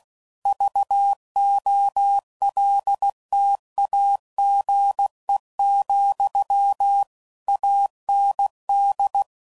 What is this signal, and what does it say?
Morse code, heard as a keyed tone, about 16 wpm, 800 Hz. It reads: VOLTAGE, AND
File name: CW Code Groups @15WPM / Technical Subjects / XC